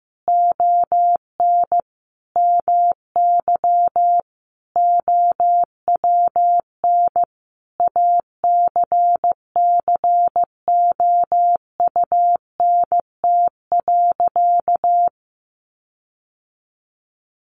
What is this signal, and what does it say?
Morse code, heard as a keyed tone, about 15 wpm, 700 Hz. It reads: ON MY OWN ACCOUNT.
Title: Sentences from Top 1400 Words 15wpm